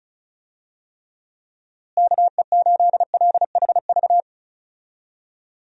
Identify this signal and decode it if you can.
Morse code, heard as a keyed tone, about 34 wpm, 700 Hz. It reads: KE8LHV
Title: USA Call Signs - 6 Characters - Repeated 3x - 35wpm